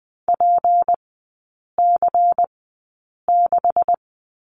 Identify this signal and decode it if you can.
Morse code, heard as a keyed tone, about 20 wpm, spaced as for 8 wpm, 690 Hz. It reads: PC6
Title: 191 Lesson 31e - Letter v - Three Characters 20wpm